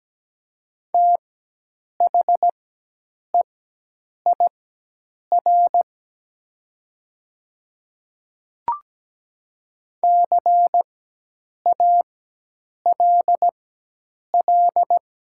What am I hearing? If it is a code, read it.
Morse code, heard as a keyed tone, about 17 wpm, spaced as for 8 wpm, 690 Hz. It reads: THEIR CALL
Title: 4x Character Spacing - Top 300 Words - 17wpm